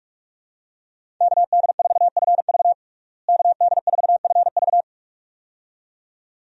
Morse code, heard as a keyed tone, about 47 wpm, 700 Hz. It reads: KD4FV KD4FV
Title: USA Call Signs - 5 Characters - Repeated 2x - 45wpm